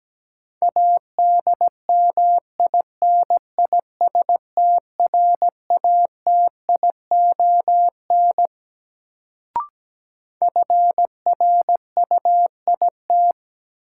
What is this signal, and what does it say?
Morse code, heard as a keyed tone, about 17 wpm, 700 Hz. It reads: ADMINISTRATION FRUIT
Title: Sentences from Top 1300 Words - New Word Drill Alt - 17wpm